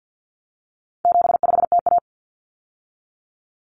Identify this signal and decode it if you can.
Morse code, heard as a keyed tone, about 50 wpm, 700 Hz. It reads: 75TA
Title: Common Cut Numbers - 4-Digit 50wpm